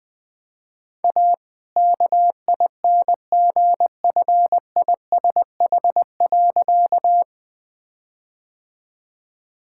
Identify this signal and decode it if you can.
Morse code, heard as a keyed tone, about 20 wpm, 700 Hz. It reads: A KINGFISH.
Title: All-Age Appropriate Jokes - 20wpm